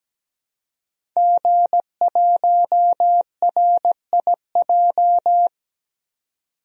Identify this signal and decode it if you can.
Morse code, heard as a keyed tone, about 17 wpm, 700 Hz. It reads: G1RIJ
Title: DXCC-100 Call Signs - 5 Characters - 17wpm